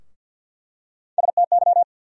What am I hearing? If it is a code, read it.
Morse code, heard as a keyed tone, about 48 wpm, 700 Hz. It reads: STY
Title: USA Call Sign Postfixes - 3 Letters 50wpm